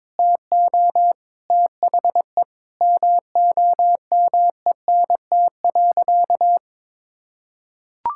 Morse code, heard as a keyed tone, about 22 wpm, 700 Hz. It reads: TO THE MOMENT.
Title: Sentences from Top 1000 Words - Review Part 6 - 22wpm